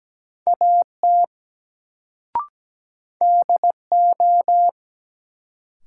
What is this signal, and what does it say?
Morse code, heard as a keyed tone, about 17 wpm, 700 Hz. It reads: AT DO
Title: Sentences from Top 100 Words - New Word Drill - 17wpm